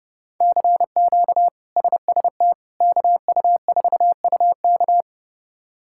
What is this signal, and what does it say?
Morse code, heard as a keyed tone, about 30 wpm, 700 Hz. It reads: CQ SST KU4UK
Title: K1USN SST QSO Practice - 30wpm